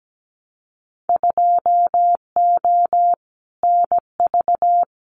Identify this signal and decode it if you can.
Morse code, heard as a keyed tone, about 17 wpm, 690 Hz. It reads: 2O NV
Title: Winter Field Day Exchanges 17wpm